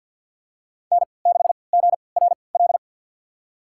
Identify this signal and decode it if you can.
Morse code, heard as a keyed tone, about 49 wpm, spaced as for 28 wpm, 700 Hz. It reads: N6GRL
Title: USA Call Signs - 3x Character Spacing - 50wpm